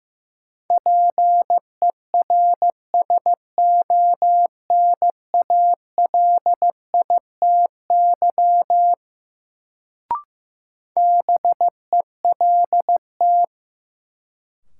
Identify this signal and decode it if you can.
Morse code, heard as a keyed tone, about 15 wpm, 700 Hz. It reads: PERSONALITY BELT
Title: Sentences from Top 1400 Words - New Word Drill Alt - 15wpm